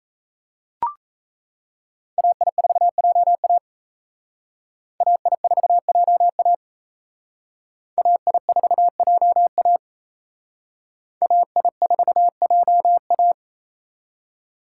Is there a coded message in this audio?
Yes: Morse code from 2.2 to 13.3 s, heard as a keyed tone, about 35 wpm, 700 Hz. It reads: AI4JA AI4JA AI4JA AI4JA